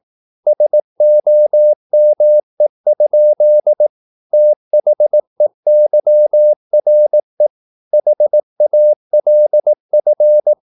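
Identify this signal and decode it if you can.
Morse code, heard as a keyed tone, about 18 wpm, 600 Hz. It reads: SOME? THEYRE HALF
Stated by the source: Morse practice at 18 wpm, English text, from CW med Gnister.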